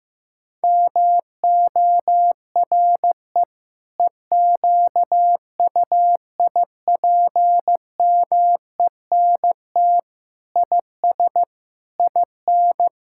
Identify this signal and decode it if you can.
Morse code, heard as a keyed tone, about 15 wpm, 700 Hz. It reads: MORE EQUIPMENT IS IN
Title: Sentences from Top 1000 Words - Review Part 5 - 15wpm